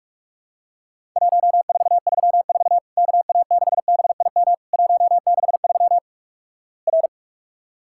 Morse code, heard as a keyed tone, about 44 wpm, 700 Hz. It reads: 1434 KA6BIM 163 R
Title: Worked All Europe DX Contest - QTC Exchange - 45wpm